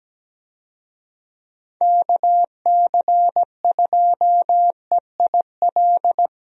Morse code, heard as a keyed tone, about 17 wpm, 700 Hz. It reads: KC2EIL
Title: USA Call Signs - 6 Characters - Repeated 2x - 17wpm